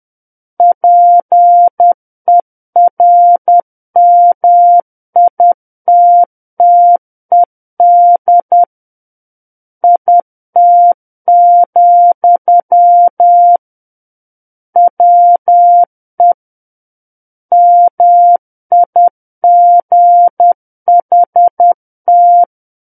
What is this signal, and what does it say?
Morse code, heard as a keyed tone, about 10 wpm, 700 Hz. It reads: PERMITTED IT, WE MIGHT